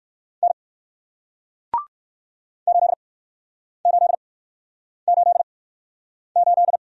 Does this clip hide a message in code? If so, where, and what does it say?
Yes: Morse code from 0.4 to 6.8 s, heard as a keyed tone, about 59 wpm, spaced as for 9 wpm, 700 Hz. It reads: N 8888